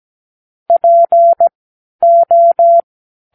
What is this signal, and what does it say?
Morse code, heard as a keyed tone, about 17 wpm, spaced as for 11 wpm, 680 Hz. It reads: PO
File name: k7qo / Practicewords